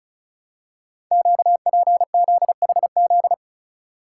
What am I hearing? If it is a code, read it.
Morse code, heard as a keyed tone, about 36 wpm, 700 Hz. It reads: QPZHZ
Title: Code Groups - 5 Letters - 35wpm